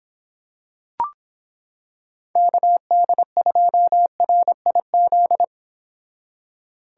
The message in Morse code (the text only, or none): KD2RIZ